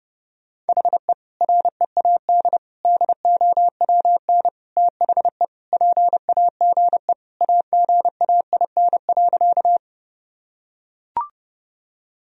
Morse code, heard as a keyed tone, about 30 wpm, 700 Hz. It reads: HE READ DOWN THE PAGE AGAIN.
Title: Sentences from Top 1000 Words - Review Part 1 - 30wpm